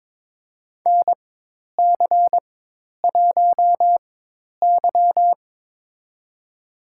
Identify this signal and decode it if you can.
Morse code, heard as a keyed tone, about 22 wpm, spaced as for 10 wpm, 700 Hz. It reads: NC1Y
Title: USA Call Signs - 4 Characters - 4x Character Spacing - 22wpm